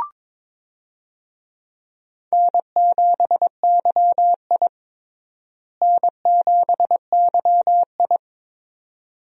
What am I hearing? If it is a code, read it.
Morse code, heard as a keyed tone, about 22 wpm, 700 Hz. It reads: N7YI N7YI
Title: USA Call Signs - 4 Characters - Repeated 2x - 22wpm